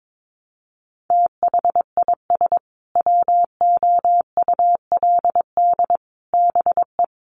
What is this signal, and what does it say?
Morse code, heard as a keyed tone, about 22 wpm, 700 Hz. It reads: THIS WOULD BE
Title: Sentences from Top 1200 Words 22wpm